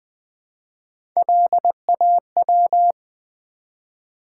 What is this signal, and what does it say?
Morse code, heard as a keyed tone, about 20 wpm, 700 Hz. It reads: LAW
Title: Build A Word - 1 Letter At A Time - 3-Letter Words - 20wpm